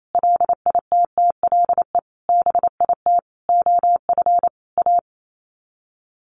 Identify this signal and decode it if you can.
Morse code, heard as a keyed tone, about 28 wpm, 700 Hz. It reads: LITTLE BIT OF A
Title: Sets of 5 Words 28wpm